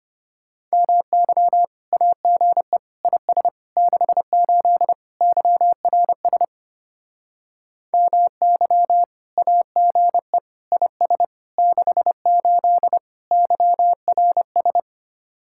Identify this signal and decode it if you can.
Morse code, heard as a keyed tone, about 33 wpm, 700 Hz. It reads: MY AGE IS 68 YRS MY AGE IS 68 YRS